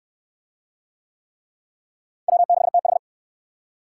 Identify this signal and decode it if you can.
Morse code, heard as a keyed tone, about 61 wpm, 700 Hz. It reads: W6TH